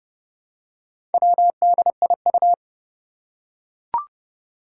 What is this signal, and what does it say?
Morse code, heard as a keyed tone, about 30 wpm, 700 Hz. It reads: WDIU
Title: Four Challenging Letter-Number Combinations 30wpm